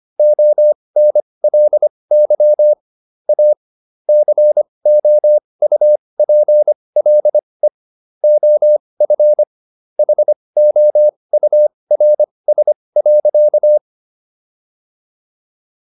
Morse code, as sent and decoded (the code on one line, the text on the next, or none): --- -. .-.. -.-- / .- / -.-. --- ..- .--. .-.. / --- ..-. / .... --- ..- .-. ... .-.-.-
ONLY A COUPLE OF HOURS.